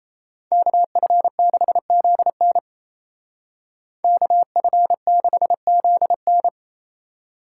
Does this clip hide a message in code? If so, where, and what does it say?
Yes: Morse code from 0.5 to 6.5 s, heard as a keyed tone, about 36 wpm, 700 Hz. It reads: KF6ZN KF6ZN